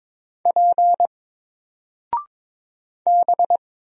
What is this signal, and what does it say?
Morse code, heard as a keyed tone, about 22 wpm, 700 Hz. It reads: P B
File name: Single-Letter-Number-22wpm